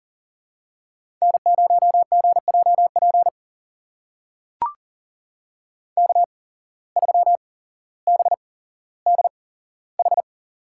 Morse code, heard as a keyed tone, about 40 wpm, spaced as for 17 wpm, 700 Hz. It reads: N0GJP K3BDH
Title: USA Call Signs - 5 Characters - 8x Character Spacing - 40wpm